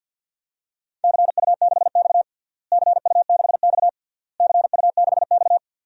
Morse code, heard as a keyed tone, about 52 wpm, 700 Hz. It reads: CU6X CU6X CU6X
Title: DXCC-100 Call Signs - 4 Characters - Repeated 3x - 50wpm